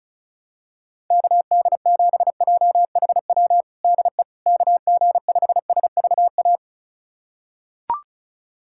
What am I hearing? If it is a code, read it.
Morse code, heard as a keyed tone, about 34 wpm, 700 Hz. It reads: KD7JHW DE KG5SVA
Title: USA Callsign DE Callsign - 6 Characters - 35wpm